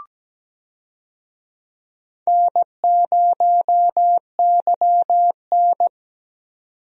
Morse code, heard as a keyed tone, about 17 wpm, 700 Hz. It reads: N0YN
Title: USA Call Signs - 4 Characters - 17wpm